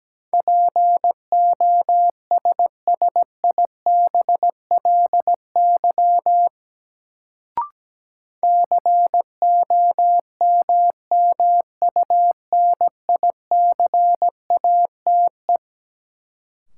Morse code, heard as a keyed tone, about 17 wpm, 700 Hz. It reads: POSSIBLY COMMUNICATE
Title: Sentences from Top 1300 Words - New Word Drill - 17wpm